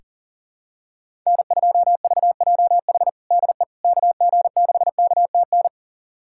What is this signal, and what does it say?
Morse code, heard as a keyed tone, about 40 wpm, 700 Hz. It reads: N2VJH DE KG6KTN
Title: USA Callsign DE Callsign 40wpm